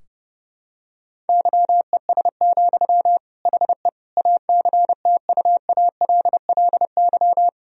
Morse code, heard as a keyed tone, about 30 wpm, 700 Hz. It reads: YES, HE ACTUALLY
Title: Sentences from Top 1000 Words - Review Part 4 - 30wpm